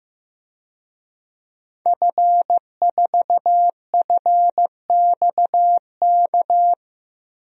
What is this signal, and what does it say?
Morse code, heard as a keyed tone, about 15 wpm, 700 Hz. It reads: F4FXK
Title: DXCC-100 Call Signs - 5 Characters - 15wpm